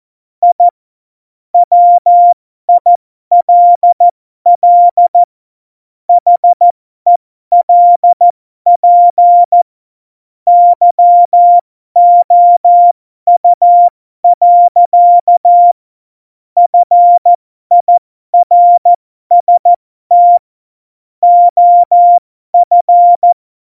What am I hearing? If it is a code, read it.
Morse code, heard as a keyed tone, about 14 wpm, spaced as for 12 wpm, 700 Hz. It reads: I WILL HELP YOU. FIRST OF